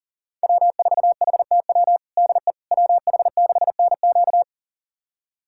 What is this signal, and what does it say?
Morse code, heard as a keyed tone, about 39 wpm, 700 Hz. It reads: W4HTW DE WH6NQ
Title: USA Callsign DE Callsign - 5 Characters - 40wpm